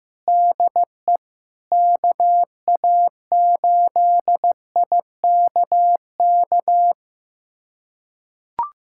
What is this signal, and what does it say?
Morse code, heard as a keyed tone, about 15 wpm, 700 Hz. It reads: DE KA8IKK
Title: USA Callsign DE Callsign - 6 Characters - 15wpm